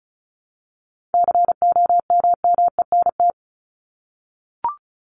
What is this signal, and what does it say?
Morse code, heard as a keyed tone, about 34 wpm, 700 Hz. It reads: COMMENT
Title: Top 1751-2000 Words 35wpm